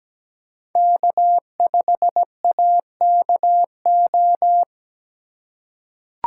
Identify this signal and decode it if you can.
Morse code, heard as a keyed tone, about 17 wpm, 700 Hz. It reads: K5AKO
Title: USA Call Signs - 5 Characters - 2x Character Spacing - 17wpm